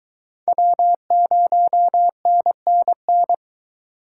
Morse code, heard as a keyed tone, about 22 wpm, 700 Hz. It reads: W0NNN